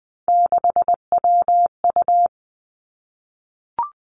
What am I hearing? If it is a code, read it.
Morse code, heard as a keyed tone, about 20 wpm, 700 Hz. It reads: 6WU
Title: Three Challenging Letter-Number Combinations 20wpm